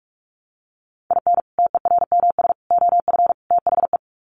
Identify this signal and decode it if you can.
Morse code, heard as a keyed tone, about 44 wpm, 700 Hz. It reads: IN TERMS OF THE
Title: Sets of 4 Words 45wpm